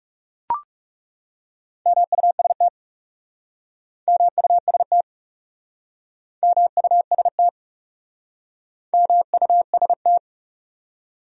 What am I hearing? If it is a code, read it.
Morse code, heard as a keyed tone, about 40 wpm, 700 Hz. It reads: MUST MUST MUST MUST